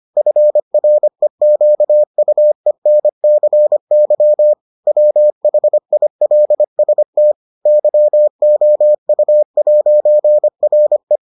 Morse code, heard as a keyed tone, about 25 wpm, 600 Hz. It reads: FREQUENCY WHILST YOU'RE